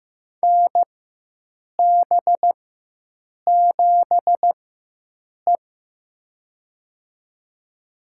Morse code, heard as a keyed tone, about 15 wpm, spaced as for 7 wpm, 700 Hz. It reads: NB7E
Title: USA Call Signs - 4 Characters - 4x Character Spacing - 15wpm